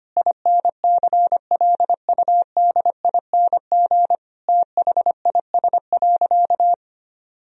Morse code, heard as a keyed tone, about 25 wpm, 700 Hz. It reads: INCLUDING THIS.
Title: Sentences from Top 1000 Words - Review Part 4 - 25wpm